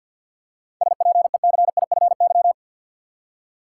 Morse code, heard as a keyed tone, about 50 wpm, 700 Hz. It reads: SPECIFY
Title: Seven Letter Words - 50wpm